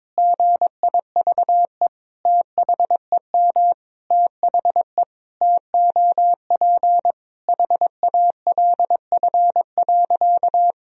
Morse code, heard as a keyed tone, about 22 wpm, 700 Hz. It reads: GIVE THEM THE TOP HALF.
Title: Sentences from Top 1000 Words - Review Part 2 - 22wpm